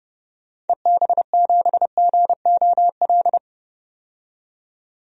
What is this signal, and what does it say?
Morse code, heard as a keyed tone, about 32 wpm, 700 Hz. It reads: EB7GOL
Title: DXCC-100 Call Signs - 6 Characters - 30wpm